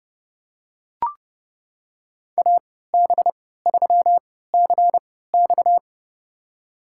Morse code, heard as a keyed tone, about 30 wpm, spaced as for 17 wpm, 700 Hz. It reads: AB3CX
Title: USA Call Signs - 3x Character Spacing - 30wpm